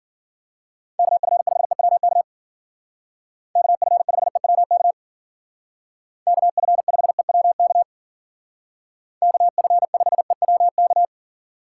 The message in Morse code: KF5EWK KF5EWK KF5EWK KF5EWK

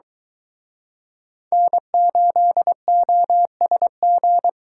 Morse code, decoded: N8OSG

